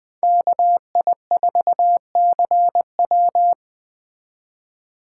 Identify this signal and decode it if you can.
Morse code, heard as a keyed tone, about 20 wpm, 700 Hz. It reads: KI4CW
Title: Speed Racing - USA Call Signs 17wpm